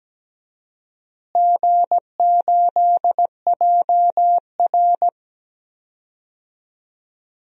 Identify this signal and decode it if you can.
Morse code, heard as a keyed tone, about 17 wpm, 700 Hz. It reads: G8JR
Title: DXCC-100 Call Signs - 4 Characters - Repeated 2x - 17wpm